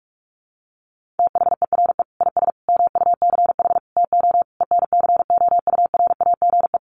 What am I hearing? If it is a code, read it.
Morse code, heard as a keyed tone, about 44 wpm, 700 Hz. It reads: THERE IS MUCH TO ENCOURAGE
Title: Sentences from Top 1400 Words 45wpm